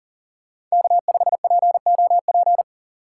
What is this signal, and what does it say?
Morse code, heard as a keyed tone, about 40 wpm, 700 Hz. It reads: K5POP
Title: USA Call Signs - 5 Characters - 3x Character Spacing - 40wpm